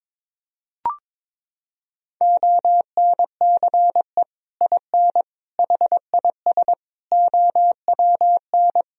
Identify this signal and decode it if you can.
Morse code, heard as a keyed tone, about 21 wpm, 700 Hz. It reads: ONCE IN HIS OWN
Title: Sentences from Top 1000 Words - Review Part 2 - 22wpm